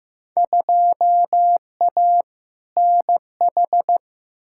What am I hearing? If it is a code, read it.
Morse code, heard as a keyed tone, about 15 wpm, 700 Hz. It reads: 2A NH
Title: Field Day Exchanges Encore - 15wpm